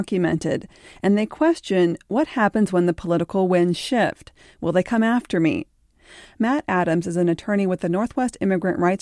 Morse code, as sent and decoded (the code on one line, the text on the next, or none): none